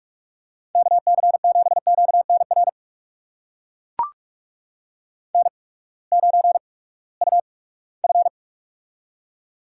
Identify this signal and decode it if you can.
Morse code, heard as a keyed tone, about 44 wpm, spaced as for 22 wpm, 700 Hz. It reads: KC7QNR N9UF
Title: USA Call Signs - 8x Character Spacing - 45wpm